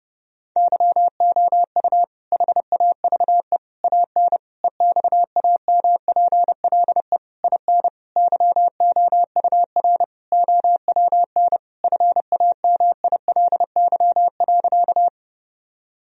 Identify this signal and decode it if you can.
Morse code, heard as a keyed tone, about 32 wpm, 700 Hz. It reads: YOU HAVE AN EXAMPLE IN YOUR OWN FAMILY.
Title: Sentences from Top 1000 Words - Review Part 1 - 30wpm